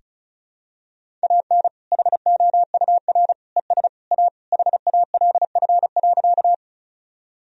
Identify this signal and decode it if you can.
Morse code, heard as a keyed tone, about 34 wpm, 700 Hz. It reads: AN HOUR ES A HALF.
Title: Sentences from Top 400 Words and Abbreviations 35wpm